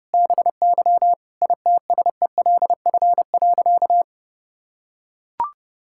Morse code, heard as a keyed tone, about 30 wpm, 700 Hz. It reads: BY ITSELF.
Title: Sentences from Top 1000 Words - Review Part 5 - 30wpm